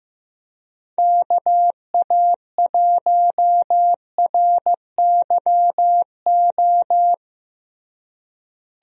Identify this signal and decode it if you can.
Morse code, heard as a keyed tone, about 15 wpm, 700 Hz. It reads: KA1RYO
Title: USA Call Signs - 6 Characters - Repeated 3x - 15wpm